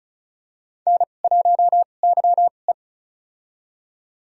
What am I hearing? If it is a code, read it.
Morse code, heard as a keyed tone, about 36 wpm, spaced as for 25 wpm, 700 Hz. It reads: N1YE
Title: USA Call Signs - 4 Characters - 2x Character Spacing - 35wpm